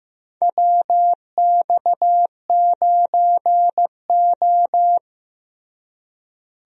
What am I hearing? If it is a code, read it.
Morse code, heard as a keyed tone, about 15 wpm, 700 Hz. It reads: WX9O
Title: USA Call Signs - 4 Characters - Repeated 2x - 15wpm